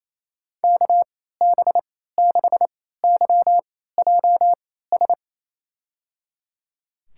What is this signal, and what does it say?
Morse code, heard as a keyed tone, about 28 wpm, spaced as for 16 wpm, 700 Hz. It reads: KB6YJS
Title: USA Call Signs - 3x Character Spacing - 28wpm